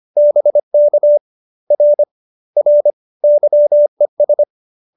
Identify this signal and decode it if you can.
Morse code, heard as a keyed tone, about 25 wpm, 570 Hz. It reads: BK R R YES